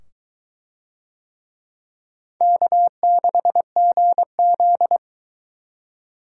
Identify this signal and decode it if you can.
Morse code, heard as a keyed tone, about 21 wpm, 700 Hz. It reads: K6GZ